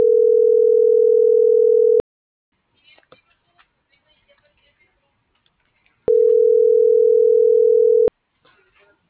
Ringback tone (North America), 440 + 480 Hz.